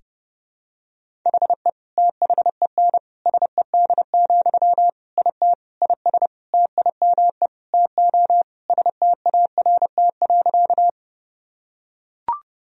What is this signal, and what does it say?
Morse code, heard as a keyed tone, about 30 wpm, 700 Hz. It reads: HE THEN SED, IT IS TIME TO START.
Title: Sentences from Top 300 Words and Abbreviations 30wpm